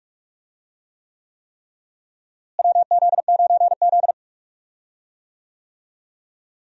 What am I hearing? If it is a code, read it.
Morse code, heard as a keyed tone, about 42 wpm, 700 Hz. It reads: WZ9Z
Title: USA Call Signs - 4 Characters - Repeated 3x - 45wpm